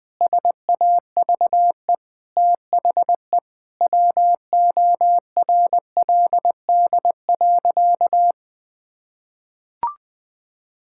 Morse code, heard as a keyed tone, about 20 wpm, 700 Hz. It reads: SAVE THE WORLD.
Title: Sentences from Top 1600 Words 20wpm